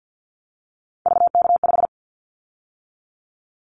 Morse code, heard as a keyed tone, about 50 wpm, 700 Hz. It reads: VK5
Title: Callsign Prefixes for Top 50 DXCC - 50wpm